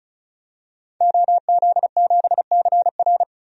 Morse code, heard as a keyed tone, about 35 wpm, 700 Hz. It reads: OZ7CR
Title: DXCC-100 Call Signs - 5 Characters - 35wpm